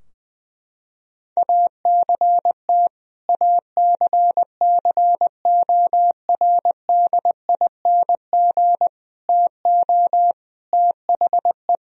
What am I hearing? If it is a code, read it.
Morse code, heard as a keyed tone, about 20 wpm, 700 Hz. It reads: ACT ACCORDING TO THE
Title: Sentences from Top 1000 Words - Review Part 4 - 20wpm